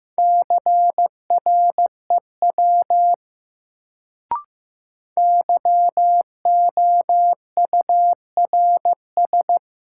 Morse code, heard as a keyed tone, about 15 wpm, 700 Hz. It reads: CREW YOURS
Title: Sentences from Top 1500 Words - New Word Drill - 15wpm